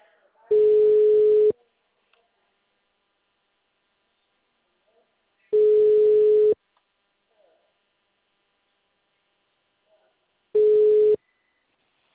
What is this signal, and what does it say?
Ringback tone (Europe, 425 Hz), with speech.
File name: out- -2046- - - .17051